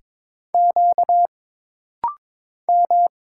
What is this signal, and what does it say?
Morse code, heard as a keyed tone, about 22 wpm, 700 Hz. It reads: Q M
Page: This rapid-fire timing is geared to push your ICR (Instant Character Recognition) skills.